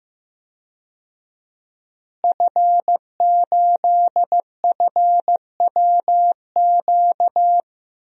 Morse code, heard as a keyed tone, about 15 wpm, 700 Hz. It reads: F8FWQ
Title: DXCC-100 Call Signs - 5 Characters - Repeated 3x - 15wpm